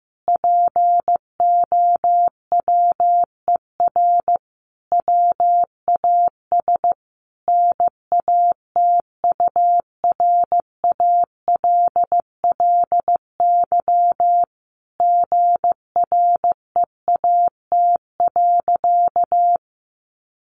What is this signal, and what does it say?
Morse code, heard as a keyed tone, about 15 wpm, 700 Hz. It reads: POWER WAS NATURALLY GREAT.
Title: Sentences from Top 1300 Words 15wpm